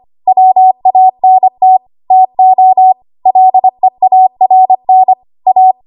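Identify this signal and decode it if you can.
Morse code, heard as a keyed tone, about 25 wpm, 750 Hz. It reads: WANT TO LEARN A
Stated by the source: ARRL 25 WPM code practice archive.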